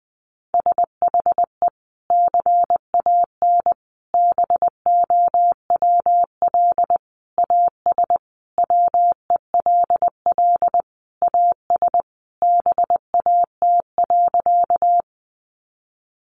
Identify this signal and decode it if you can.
Morse code, heard as a keyed tone, about 20 wpm, 700 Hz. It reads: SHE CAN BOWL AS WELL AS BAT.
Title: Sentences from Top 1500 Words 20wpm